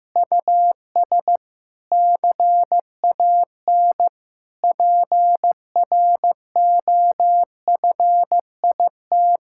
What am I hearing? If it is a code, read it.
Morse code, heard as a keyed tone, about 15 wpm, 700 Hz. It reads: US CAN PROFIT
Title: Sentences from Top 1000 Words - Review Part 5 - 15wpm